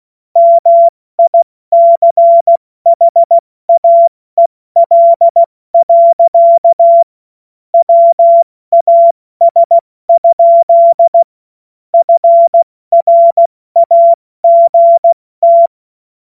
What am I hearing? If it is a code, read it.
Morse code, heard as a keyed tone, about 16 wpm, spaced as for 14 wpm, 680 Hz. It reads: MICHAEL. WAS? FRAGT